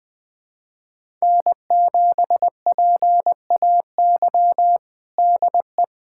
Morse code, heard as a keyed tone, about 20 wpm, 700 Hz. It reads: N7PAY DE
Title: USA Callsign DE Callsign - 5 Characters - 20wpm